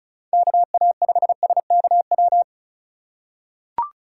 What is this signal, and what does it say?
Morse code, heard as a keyed tone, about 36 wpm, 700 Hz. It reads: KA5SKW
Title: USA Call Signs - 6 Characters - 2x Character Spacing - 35wpm